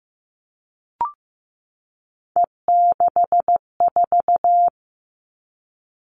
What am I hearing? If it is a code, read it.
Morse code, heard as a keyed tone, about 15 wpm, 700 Hz. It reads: E64
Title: Three Challenging Letter-Number Combinations 15wpm